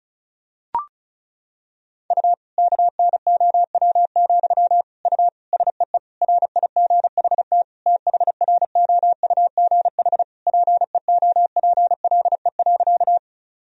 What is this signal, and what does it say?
Morse code, heard as a keyed tone, about 36 wpm, 700 Hz. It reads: U KNOW, U SEE RIGHT THROUGH PEOPLE.
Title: Sentences from Top 200 Words and Abbreviations 35wpm